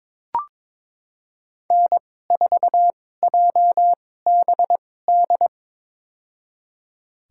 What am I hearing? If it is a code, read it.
Morse code, heard as a keyed tone, about 22 wpm, spaced as for 16 wpm, 700 Hz. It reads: N4JBD
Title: USA Call Signs - 5 Characters - 2x Character Spacing - 22wpm